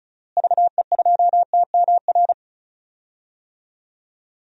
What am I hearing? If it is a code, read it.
Morse code, heard as a keyed tone, about 36 wpm, 700 Hz. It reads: VE2TMR
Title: DXCC-100 Call Signs - 6 Characters - Repeated 3x - 35wpm